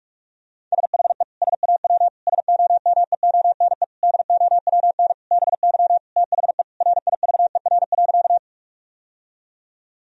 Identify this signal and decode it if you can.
Morse code, heard as a keyed tone, about 42 wpm, 700 Hz. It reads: SHE SAW SOMEONE DOWN BY THE RIVER.
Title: Sentences from Top 1000 Words - Review Part 4 - 45wpm